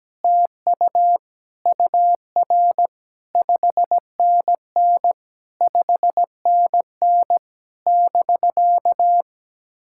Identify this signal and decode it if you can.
Morse code, heard as a keyed tone, about 17 wpm, 700 Hz. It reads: TU UR 5NN 5NN <BK>
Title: SOTA QSO Practice 17wpm